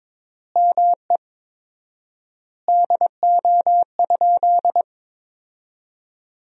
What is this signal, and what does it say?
Morse code, heard as a keyed tone, about 21 wpm, 700 Hz. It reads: ME DO?
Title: 4x Word-Spacing - Sentences from Top 200 Words - 22wpm